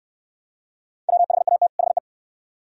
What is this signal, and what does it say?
Morse code, heard as a keyed tone, about 64 wpm, 700 Hz. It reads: WHAT HE